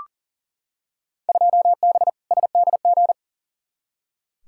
Morse code, heard as a keyed tone, about 40 wpm, 700 Hz. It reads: 2B SDG
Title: Field Day Exchanges Encore - 40wpm